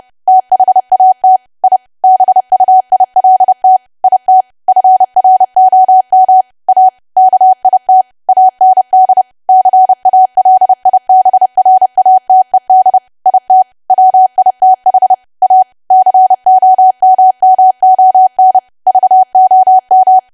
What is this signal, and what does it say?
Morse code, heard as a keyed tone, about 30 wpm, 750 Hz. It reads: THAT I BUILT IT FROM A KIT AND CALIBRATED IT WITH A COMMON VOM
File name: cwa / W1AW-Level3 / 30WPM